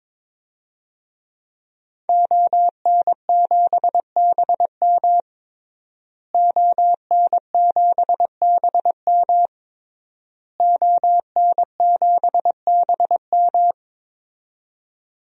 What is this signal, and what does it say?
Morse code, heard as a keyed tone, about 22 wpm, 700 Hz. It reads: ON7BM ON7BM ON7BM
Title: DXCC-100 Call Signs - 5 Characters - Repeated 3x - 22wpm